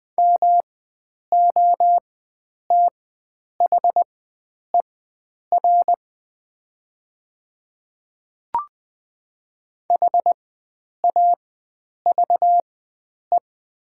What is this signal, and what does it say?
Morse code, heard as a keyed tone, about 20 wpm, spaced as for 9 wpm, 700 Hz. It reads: MOTHER HAVE
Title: 4x Character Spacing - Top 300 Words - 20wpm